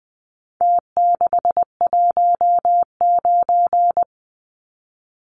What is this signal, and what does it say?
Morse code, heard as a keyed tone, about 20 wpm, 700 Hz. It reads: T619
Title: Numbers - Contest Style - 4-Digit 20wpm